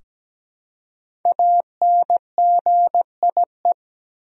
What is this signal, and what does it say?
Morse code, heard as a keyed tone, about 17 wpm, 700 Hz. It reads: ANGIE
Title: USA Common Names - 17wpm